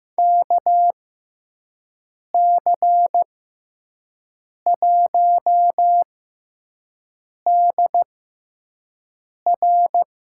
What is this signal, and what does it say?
Morse code, heard as a keyed tone, about 15 wpm, spaced as for 5 wpm, 700 Hz. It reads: KC1DR
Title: USA Call Signs - 6 Characters - 6x Character Spacing - 15wpm